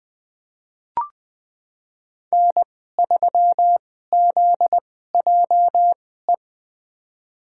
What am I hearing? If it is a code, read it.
Morse code, heard as a keyed tone, about 20 wpm, spaced as for 14 wpm, 700 Hz. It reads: N3ZJE
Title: USA Call Signs - 5 Characters - 2x Character Spacing - 20wpm